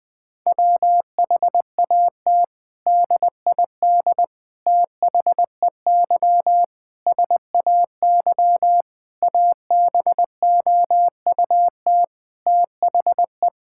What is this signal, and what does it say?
Morse code, heard as a keyed tone, about 21 wpm, 700 Hz. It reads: WHAT DID THEY SAY ABOUT THE
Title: Sentences from Top 1000 Words - Review Part 5 - 20wpm